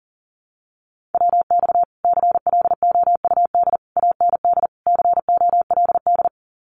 Morse code, heard as a keyed tone, about 40 wpm, 700 Hz. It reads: WX CLOUD AND COLD
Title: Common QSO Phrases Encore 40wpm